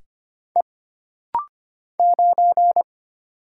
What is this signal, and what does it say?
Morse code, heard as a keyed tone, about 25 wpm, 700 Hz. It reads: E 9
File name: Single Character (Letter:Number:Punctuation:Prosign) - Rapid-Fire - 25wpm